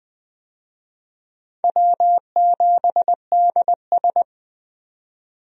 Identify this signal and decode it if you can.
Morse code, heard as a keyed tone, about 20 wpm, 690 Hz. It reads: W7DS
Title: USA Call Signs - 4 Characters - Repeated 3x - 20wpm